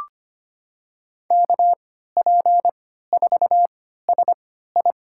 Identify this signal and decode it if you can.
Morse code, heard as a keyed tone, about 25 wpm, spaced as for 14 wpm, 700 Hz. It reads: KP4SI
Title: USA Call Signs - 5 Characters - 3x Character Spacing - 25wpm